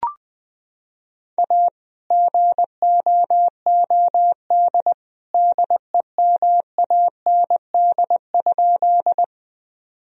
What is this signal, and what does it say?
Morse code, heard as a keyed tone, about 20 wpm, 700 Hz. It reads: A GOOD DEMAND?
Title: Sentences from Top 1000 Words - Review Part 4 - 20wpm